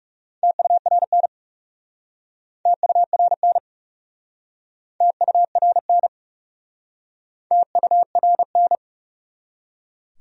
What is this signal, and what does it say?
Morse code, heard as a keyed tone, about 40 wpm, 700 Hz. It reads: TURN TURN TURN TURN